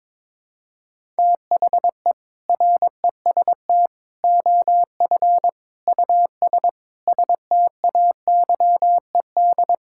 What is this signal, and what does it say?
Morse code, heard as a keyed tone, about 21 wpm, 700 Hz. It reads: THE REST OF US STAYED
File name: practice / 22wpm / Sentences with 4-Word Phrases - 22wpm